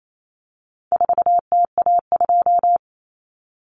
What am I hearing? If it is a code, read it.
Morse code, heard as a keyed tone, about 28 wpm, 700 Hz. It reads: 4TA2
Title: Common Cut Numbers - 4-Digit 28wpm